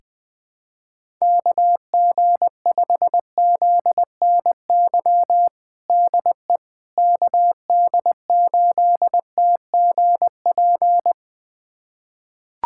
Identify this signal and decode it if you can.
Morse code, heard as a keyed tone, about 20 wpm, 700 Hz. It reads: KG5ZNY DE KD8TGP
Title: USA Callsign DE Callsign - 6 Characters - 20wpm